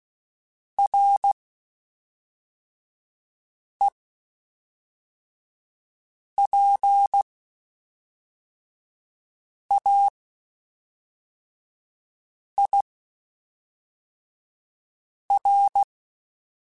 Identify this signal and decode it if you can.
Morse code, heard as a keyed tone, about 16 wpm, spaced as for 3 wpm, 800 Hz. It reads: REPAIR
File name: Quotes Spaced / Quotes Spaced-mp3s / Airtight sp @5